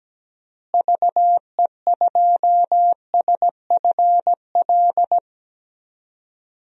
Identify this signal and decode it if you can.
Morse code, heard as a keyed tone, about 17 wpm, 700 Hz. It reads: VE2SFL
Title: DXCC-100 Call Signs - 6 Characters - Repeated 3x - 17wpm